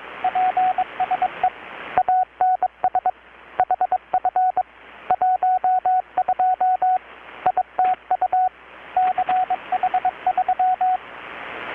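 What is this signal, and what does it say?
Morse code, heard as a keyed tone, about 22 wpm, 740 Hz, with radio noise. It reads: PSE ANS HF 12 ITU CH3
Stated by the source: Morse from KPH's Night of Nights XX maritime radiotelegraphy event, received in Chiapas, Mexico about 3,800 km away on a Kenwood R-600 with a grounded 10 m long-wire antenna.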